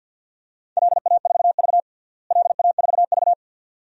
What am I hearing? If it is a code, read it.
Morse code, heard as a keyed tone, about 53 wpm, 700 Hz. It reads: LA4V LA4V
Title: DXCC-100 Call Signs - 4 Characters - Repeated 2x - 50wpm